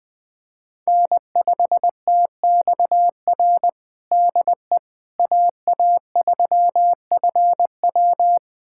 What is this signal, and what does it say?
Morse code, heard as a keyed tone, about 21 wpm, 700 Hz. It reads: N5TXR DE AA3FW
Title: USA Callsign DE Callsign - 5 Characters - 20wpm